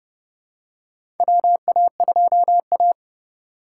Morse code, heard as a keyed tone, about 30 wpm, 700 Hz. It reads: WA2A
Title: USA Call Signs - 4 Characters - 3x Character Spacing - 30wpm